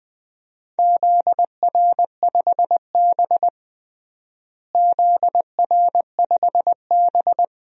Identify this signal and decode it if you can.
Morse code, heard as a keyed tone, about 20 wpm, 700 Hz. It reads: ZR5B ZR5B
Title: DXCC-100 Call Signs - 4 Characters - Repeated 3x - 20wpm